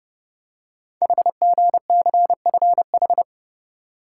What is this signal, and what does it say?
Morse code, heard as a keyed tone, about 29 wpm, 690 Hz. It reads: HGCFH
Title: Code Groups - 5 Letters - 30wpm